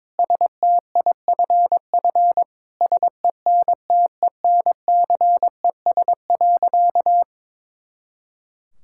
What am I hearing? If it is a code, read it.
Morse code, heard as a keyed tone, about 22 wpm, 700 Hz. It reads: STIFF SENTENCES.
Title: Pangram Challenge v2 - 22wpm